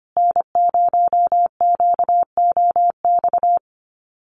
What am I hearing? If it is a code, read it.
Morse code, heard as a keyed tone, about 25 wpm, 700 Hz. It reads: N0QOX
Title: USA Call Signs Repeated 3x 25wpm